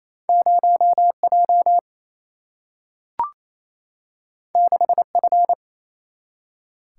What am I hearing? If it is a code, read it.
Morse code, heard as a keyed tone, about 28 wpm, 700 Hz. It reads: 0 J 6 F
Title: Single-Letter Single-Number 28wpm